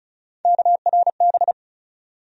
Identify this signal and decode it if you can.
Morse code, heard as a keyed tone, about 34 wpm, 700 Hz. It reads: KRB
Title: USA Call Sign Postfixes - 3 Letters 35wpm